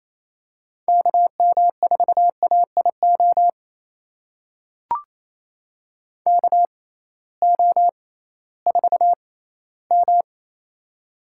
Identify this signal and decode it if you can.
Morse code, heard as a keyed tone, about 28 wpm, spaced as for 16 wpm, 700 Hz. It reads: KM4AIO KO4M
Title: USA Call Signs - 6x Character Spacing - 28wpm